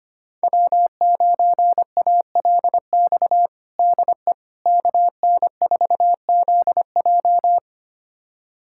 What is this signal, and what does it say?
Morse code, heard as a keyed tone, about 25 wpm, 700 Hz. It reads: W9ALX DE KN4ZJ
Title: USA Callsign DE Callsign - 5 Characters - 25wpm